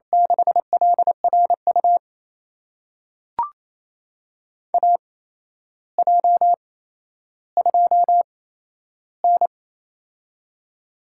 Morse code, heard as a keyed tone, about 28 wpm, spaced as for 12 wpm, 700 Hz. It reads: 6LRU AJ2N
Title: USA Call Signs - 8x Character Spacing - 28wpm